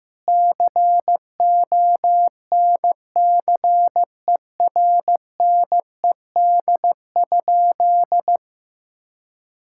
Morse code, heard as a keyed tone, about 15 wpm, 700 Hz. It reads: CONCERNED?
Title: Sentences from Top 1000 Words - Review Part 6 - 15wpm